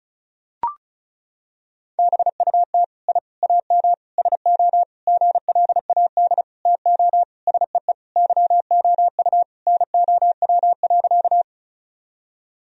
Morse code, heard as a keyed tone, about 36 wpm, 700 Hz. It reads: BUT I AM SO GLAD TO SEE YOU NOW.
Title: Sentences from Top 1000 Words - Review Part 2 - 35wpm